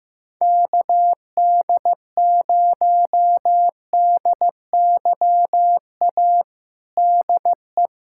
Morse code, heard as a keyed tone, about 15 wpm, 700 Hz. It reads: KD0DYA DE
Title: USA Callsign DE Callsign 15wpm